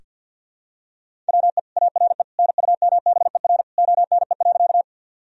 Morse code, heard as a keyed tone, about 46 wpm, 700 Hz. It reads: WE ARE NUMBER ONE.
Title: Sentences from Top 1000 Words - Review Part 1 - 50wpm